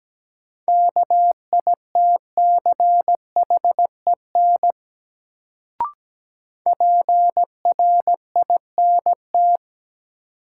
Morse code, heard as a keyed tone, about 17 wpm, 700 Hz. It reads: KITCHEN PRINT
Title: Sentences from Top 1400 Words - New Word Drill Alt - 17wpm